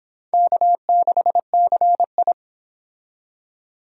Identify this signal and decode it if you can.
Morse code, heard as a keyed tone, about 29 wpm, 700 Hz. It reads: K6CI